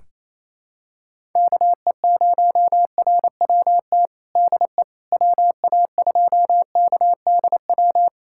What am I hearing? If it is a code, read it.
Morse code, heard as a keyed tone, about 27 wpm, 700 Hz. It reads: KE0RWT DE WA2KDW
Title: USA Callsign DE Callsign - 6 Characters - 28wpm